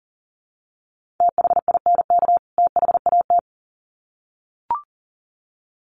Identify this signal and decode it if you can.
Morse code, heard as a keyed tone, about 40 wpm, 700 Hz. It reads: THINK THAT
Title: Sets of 2 Words 40wpm